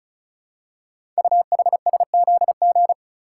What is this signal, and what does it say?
Morse code, heard as a keyed tone, about 36 wpm, 700 Hz. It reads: UHSZG
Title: Code Groups - 5 Letters - 35wpm